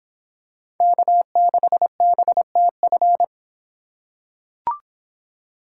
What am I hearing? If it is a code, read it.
Morse code, heard as a keyed tone, about 24 wpm, 700 Hz. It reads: K6BTF